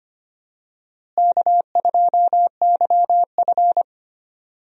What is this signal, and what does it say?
Morse code, heard as a keyed tone, about 25 wpm, 700 Hz. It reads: K2YF
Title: USA Call Signs - 4 Characters - Repeated 2x - 25wpm